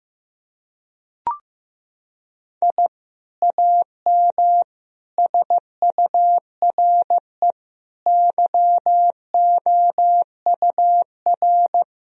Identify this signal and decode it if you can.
Morse code, heard as a keyed tone, about 15 wpm, 700 Hz. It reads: I AM SURE YOUR
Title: Sentences from Top 1000 Words - Review Part 2 - 15wpm